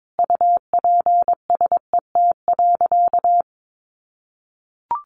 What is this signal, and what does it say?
Morse code, heard as a keyed tone, about 22 wpm, 700 Hz. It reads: UPSET.
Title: Sentences from Top 1200 Words 22wpm